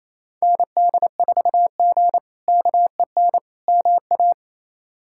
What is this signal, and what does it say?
Morse code, heard as a keyed tone, about 27 wpm, 700 Hz. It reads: ND4G KEN MA
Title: K1USN SST QSO Practice - 28wpm